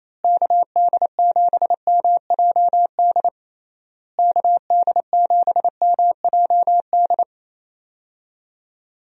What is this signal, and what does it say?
Morse code, heard as a keyed tone, about 28 wpm, 700 Hz. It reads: KD7MJD KD7MJD
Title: USA Call Signs - 6 Characters - Repeated 3x - 28wpm